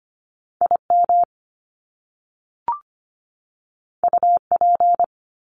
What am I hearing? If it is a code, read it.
Morse code, heard as a keyed tone, about 25 wpm, 700 Hz. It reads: IM UP
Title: Top Two Letter Combinations in Words 25wpm